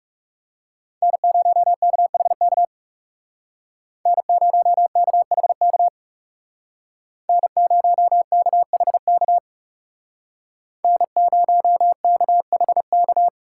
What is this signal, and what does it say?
Morse code, heard as a keyed tone, about 37 wpm, 700 Hz. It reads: N0KHK N0KHK N0KHK N0KHK
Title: Speed Racing - USA Call Signs 30wpm